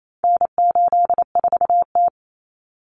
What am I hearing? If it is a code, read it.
Morse code, heard as a keyed tone, about 28 wpm, 700 Hz. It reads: N84T
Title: Common Cut Numbers - 4-Digit 28wpm